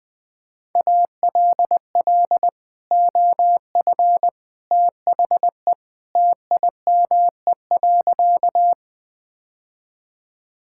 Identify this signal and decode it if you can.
Morse code, heard as a keyed tone, about 20 wpm, 700 Hz. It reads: ALL OF THE TIME.
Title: Sentences with 3-Word Phrases - Spoken Beginning - 20wpm